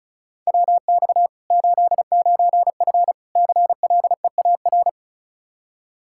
Morse code, heard as a keyed tone, about 36 wpm, 700 Hz. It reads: WX 89F CLEAR
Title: Practice QSO - Exchange 2 - Info - 35wpm